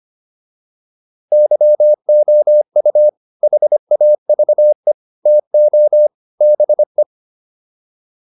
Morse code, heard as a keyed tone, about 25 wpm, 600 Hz. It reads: YOU HAVE TO BE
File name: WD304 25